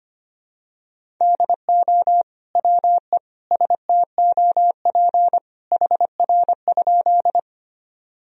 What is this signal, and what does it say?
Morse code, heard as a keyed tone, about 25 wpm, 700 Hz. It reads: DO WE STOP HR?
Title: Sentences from Top 300 Words and Abbreviations 25wpm